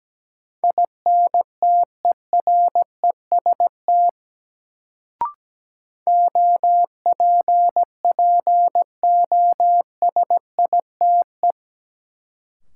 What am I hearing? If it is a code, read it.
Morse code, heard as a keyed tone, about 17 wpm, 700 Hz. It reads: INTEREST OPPOSITE
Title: Sets of 5 Words Encore - New Word Drill - 17wpm